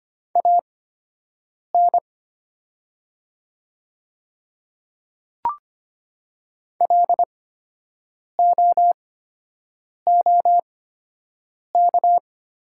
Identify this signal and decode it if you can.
Morse code, heard as a keyed tone, about 25 wpm, spaced as for 7 wpm, 700 Hz. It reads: AN LOOK
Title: 8x Character Spacing - Top 100 Words - 25wpm